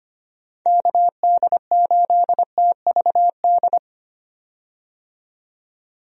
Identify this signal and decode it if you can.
Morse code, heard as a keyed tone, about 25 wpm, 700 Hz. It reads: KD8TVD
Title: USA Call Signs - 6 Characters - 25wpm